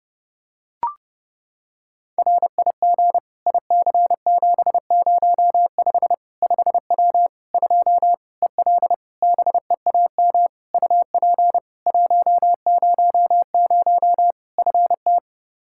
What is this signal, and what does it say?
Morse code, heard as a keyed tone, about 30 wpm, 700 Hz. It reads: RIG IC705 5W 2 EL BEAM UP 100 FT
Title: K1USN SST QSO Practice 30wpm